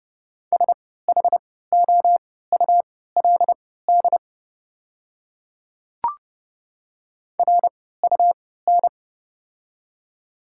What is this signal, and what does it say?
Morse code, heard as a keyed tone, about 30 wpm, spaced as for 17 wpm, 700 Hz. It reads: SHOULD RUN
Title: 3x Character Spacing - Top 300 Words - 30wpm